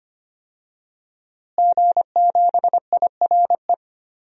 Morse code, heard as a keyed tone, about 25 wpm, 700 Hz. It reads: G7IRE
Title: DXCC-100 Call Signs - 5 Characters - Repeated 2x - 25wpm